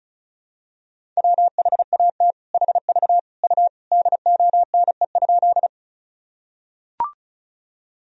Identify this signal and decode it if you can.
Morse code, heard as a keyed tone, about 34 wpm, 700 Hz. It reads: WHAT HV U DONE?
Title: Sentences from Top 500 Words and Abbreviations 35wpm